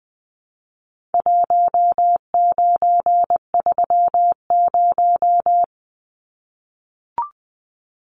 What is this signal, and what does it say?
Morse code, heard as a keyed tone, about 20 wpm, 700 Hz. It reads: 1930
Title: Numbers - Contest Style - 4-Digit 20wpm